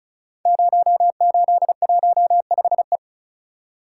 Morse code, heard as a keyed tone, about 36 wpm, 700 Hz. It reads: 0815E
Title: Code Groups - 5 Letter:Numbers - 35wpm